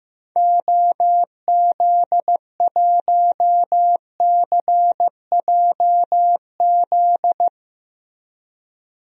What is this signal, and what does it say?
Morse code, heard as a keyed tone, about 15 wpm, 700 Hz. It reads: OZ1CJZ
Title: DXCC-100 Call Signs - 6 Characters - Repeated 3x - 15wpm